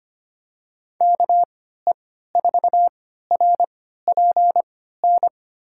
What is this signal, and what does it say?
Morse code, heard as a keyed tone, about 25 wpm, spaced as for 14 wpm, 700 Hz. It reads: KE4RPN
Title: USA Call Signs - 3x Character Spacing - 25wpm